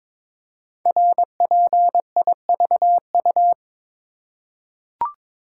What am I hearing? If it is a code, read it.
Morse code, heard as a keyed tone, about 22 wpm, 700 Hz. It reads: RPIVU
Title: Code Groups - 5 Letters - 22wpm